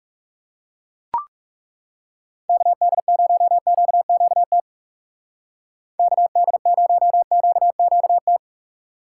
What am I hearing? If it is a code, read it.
Morse code, heard as a keyed tone, about 37 wpm, 700 Hz. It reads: KD0QQT KD0QQT
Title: Speed Racing - USA 2x3 Call Signs 30wpm